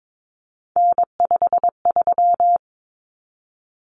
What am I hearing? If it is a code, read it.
Morse code, heard as a keyed tone, about 22 wpm, 700 Hz. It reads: N53
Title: Common Cut Numbers - 3-Digit 22wpm